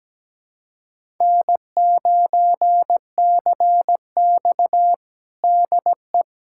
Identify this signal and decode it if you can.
Morse code, heard as a keyed tone, about 17 wpm, 700 Hz. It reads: N9CX DE
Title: USA Callsign DE Callsign - 4 Characters - 17wpm